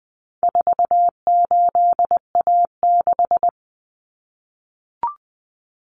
Morse code, heard as a keyed tone, about 20 wpm, 700 Hz. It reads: 48A6
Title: Common Cut Numbers - 4-Digit 20wpm